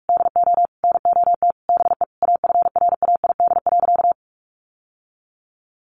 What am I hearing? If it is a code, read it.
Morse code, heard as a keyed tone, about 44 wpm, 700 Hz. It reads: DO NOT BE AFRAID.
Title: Sentences from Top 1500 Words 45wpm